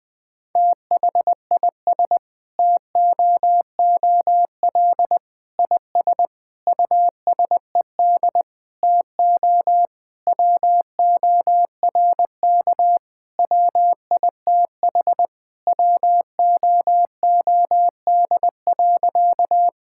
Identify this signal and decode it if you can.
Morse code, heard as a keyed tone, about 20 wpm, 700 Hz. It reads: THIS TOOL IS USED TO WORK WITH WOOD.
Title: Sentences from Top 1000 Words - Review Part 6 - 20wpm